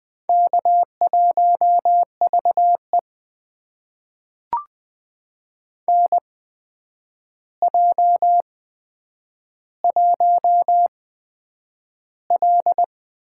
Morse code, heard as a keyed tone, about 20 wpm, spaced as for 9 wpm, 700 Hz. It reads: K1VE NJ1L
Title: USA Call Signs - 4 Characters - 8x Character Spacing - 20wpm